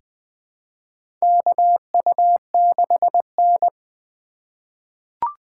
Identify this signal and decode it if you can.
Morse code, heard as a keyed tone, about 20 wpm, 700 Hz. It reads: KU6N
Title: USA Call Signs - 4 Characters - 2x Character Spacing - 20wpm